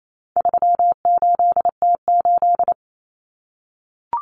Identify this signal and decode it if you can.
Morse code, heard as a keyed tone, about 28 wpm, 700 Hz. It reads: 38T8
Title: Common Cut Numbers - 4-Digit 28wpm